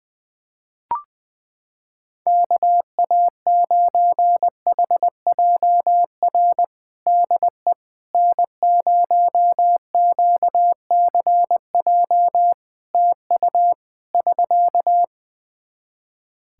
Morse code, heard as a keyed tone, about 21 wpm, 700 Hz. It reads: KA9HJR DE N0QCJ TU <SK>
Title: Practice QSO - Exchange 4 - Ending - 20wpm